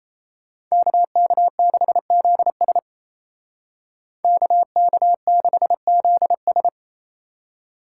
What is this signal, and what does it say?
Morse code, heard as a keyed tone, about 32 wpm, 700 Hz. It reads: KK6ZS KK6ZS
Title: Speed Racing - USA 2x2 Call Signs 28wpm